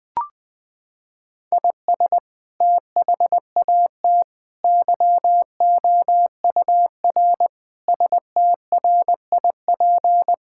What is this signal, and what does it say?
Morse code, heard as a keyed tone, about 21 wpm, 700 Hz. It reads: IS THAT YOUR STRIP
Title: Sentences from Top 1600 Words 20wpm